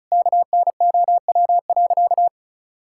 Morse code, heard as a keyed tone, about 36 wpm, 700 Hz. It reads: KNOW.
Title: Sentences from Top 1000 Words - Review Part 5 - 35wpm